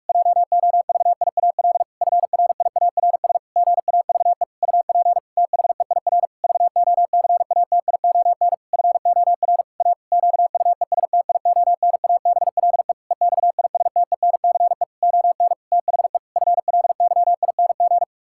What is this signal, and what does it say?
Morse code, heard as a keyed tone, about 46 wpm, 700 Hz. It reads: JOVIAL FRIARS GAVE UP THEIR VOCATION FOR A QUESTIONABLE EXISTENCE ON THE FLYING
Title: Pangram Challenge v2 - 45wpm